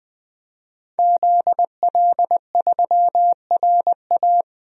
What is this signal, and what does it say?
Morse code, heard as a keyed tone, about 20 wpm, 700 Hz. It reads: ZL3RA
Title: DXCC-100 Call Signs - 5 Characters - 20wpm